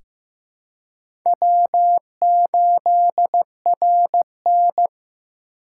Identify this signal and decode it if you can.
Morse code, heard as a keyed tone, about 15 wpm, 700 Hz. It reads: W8RN
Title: USA Call Signs - 4 Characters - 3x Character Spacing - 15wpm